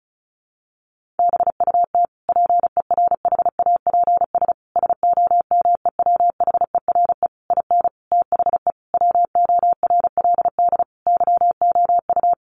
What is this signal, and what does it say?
Morse code, heard as a keyed tone, about 34 wpm, 700 Hz. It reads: BUT PERHAPS SOMEWHERE IN THE WORLD YOU
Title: Sentences from Top 1400 Words 35wpm